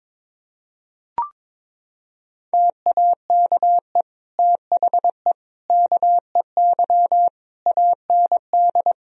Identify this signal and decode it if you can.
Morse code, heard as a keyed tone, about 22 wpm, 700 Hz. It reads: TAKE THE KEY AND
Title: Sentences from Top 1000 Words - Review Part 4 - 22wpm